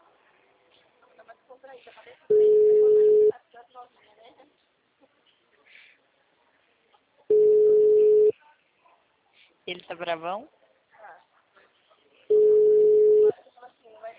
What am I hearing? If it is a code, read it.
Ringback tone (Europe, 425 Hz), with speech.
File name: out- -2036